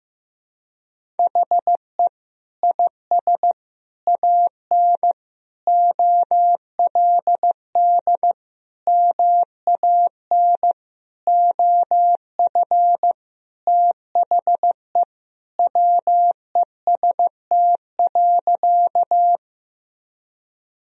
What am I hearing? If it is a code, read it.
Morse code, heard as a keyed tone, about 15 wpm, 700 Hz. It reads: HE IS AN OLD MAN OF THE WEST.
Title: Sentences from Top 1000 Words - Review Part 2 - 15wpm